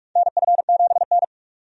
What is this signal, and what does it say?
Morse code, heard as a keyed tone, about 42 wpm, 700 Hz. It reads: NF7N
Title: USA Call Signs - 4 Characters - 45wpm